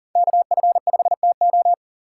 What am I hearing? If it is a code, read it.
Morse code, heard as a keyed tone, about 40 wpm, 700 Hz. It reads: KF5TO
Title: USA Call Signs - 5 Characters - 3x Character Spacing - 40wpm